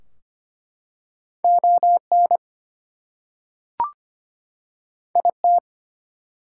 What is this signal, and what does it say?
Morse code, heard as a keyed tone, about 25 wpm, 700 Hz. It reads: ON IT
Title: Two Letter Words - 25wpm